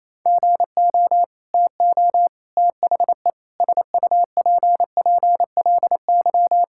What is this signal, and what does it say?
Morse code, heard as a keyed tone, about 27 wpm, 700 Hz. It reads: GO TO THE SUPPLY
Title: Sentences from Top 1000 Words - Review Part 5 - 28wpm